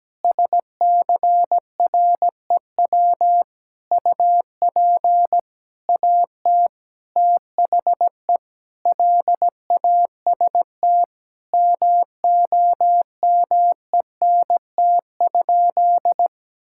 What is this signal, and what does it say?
Morse code, heard as a keyed tone, about 17 wpm, 700 Hz. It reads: SCREW UP AT THE LAST MOMENT?
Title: Sentences from Top 1600 Words 17wpm